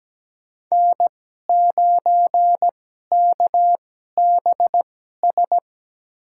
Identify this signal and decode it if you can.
Morse code, heard as a keyed tone, about 17 wpm, spaced as for 12 wpm, 700 Hz. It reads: N9KBS
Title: USA Call Signs - 5 Characters - 2x Character Spacing - 17wpm